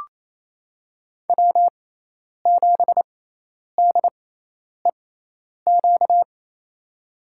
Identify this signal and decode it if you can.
Morse code, heard as a keyed tone, about 28 wpm, spaced as for 10 wpm, 700 Hz. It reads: W7DEQ
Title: USA Call Signs - 5 Characters - 6x Character Spacing - 28wpm